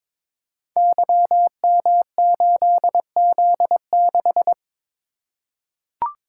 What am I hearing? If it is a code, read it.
Morse code, heard as a keyed tone, about 22 wpm, 700 Hz. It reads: YM8Z6
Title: Code Groups - 5 Letter:Numbers - 22wpm